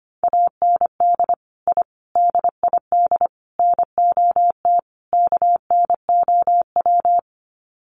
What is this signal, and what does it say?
Morse code, heard as a keyed tone, about 25 wpm, 690 Hz. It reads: AND I DID NOT KNOW
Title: Sets of 5 Words Encore 25wpm